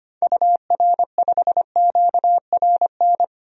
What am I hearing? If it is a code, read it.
Morse code, heard as a keyed tone, about 25 wpm, 700 Hz. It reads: UR5QRN
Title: DXCC-100 Call Signs - 6 Characters - 25wpm